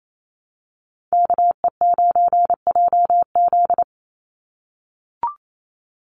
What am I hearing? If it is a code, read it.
Morse code, heard as a keyed tone, about 28 wpm, 700 Hz. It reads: KE9JZ
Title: USA Call Signs 28wpm